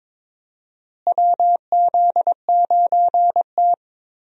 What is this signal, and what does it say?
Morse code, heard as a keyed tone, about 22 wpm, 700 Hz. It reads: WZ9T
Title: USA Call Signs - 4 Characters - Repeated 2x - 22wpm